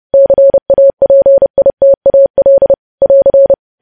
Morse code, heard as a keyed tone, about 30 wpm, 550 Hz. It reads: CAPITAL <AR>
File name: podcast / News headlines-1x-30x30